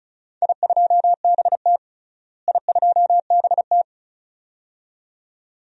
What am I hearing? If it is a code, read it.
Morse code, heard as a keyed tone, about 36 wpm, 700 Hz. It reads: I2BT I2BT
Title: DXCC-100 Call Signs - 4 Characters - Repeated 2x - 35wpm